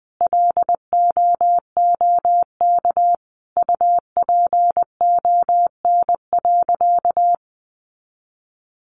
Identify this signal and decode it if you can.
Morse code, heard as a keyed tone, about 20 wpm, 700 Hz. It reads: LOOK UPON.
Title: Sentences from Top 1400 Words 20wpm